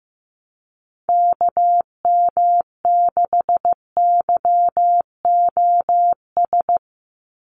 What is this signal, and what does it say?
Morse code, heard as a keyed tone, about 15 wpm, 700 Hz. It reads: KM6YOS
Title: USA Call Signs 15wpm